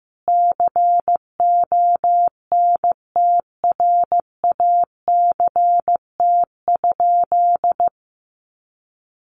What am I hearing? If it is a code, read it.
Morse code, heard as a keyed tone, about 15 wpm, 700 Hz. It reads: CONTRACT?
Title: Sentences from Top 1400 Words 15wpm